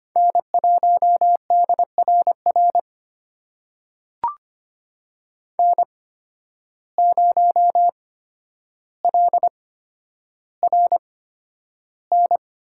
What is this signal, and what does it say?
Morse code, heard as a keyed tone, about 25 wpm, spaced as for 11 wpm, 700 Hz. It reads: N1DRR N0LRN
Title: USA Call Signs - 5 Characters - 8x Character Spacing - 25wpm